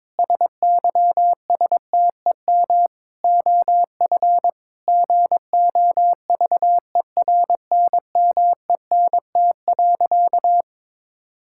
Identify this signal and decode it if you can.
Morse code, heard as a keyed tone, about 21 wpm, 700 Hz. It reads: SYSTEM OF GOVERNMENT.
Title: Sentences from Top 1000 Words - Review Part 6 - 22wpm